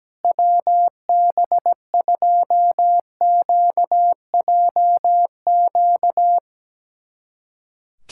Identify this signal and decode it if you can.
Morse code, heard as a keyed tone, about 17 wpm, 700 Hz. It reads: WB2QJQ
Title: USA Callsign DE Callsign - 6 Characters - 17wpm